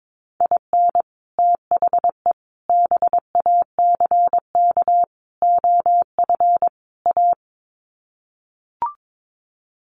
Morse code, heard as a keyed tone, about 22 wpm, 700 Hz. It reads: IN THE BACK OF A
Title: Sets of 5 Words Encore2 22wpm